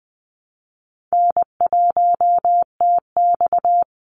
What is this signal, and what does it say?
Morse code, heard as a keyed tone, about 19 wpm, 700 Hz. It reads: N1TX
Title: Easy Call Signs - 20wpm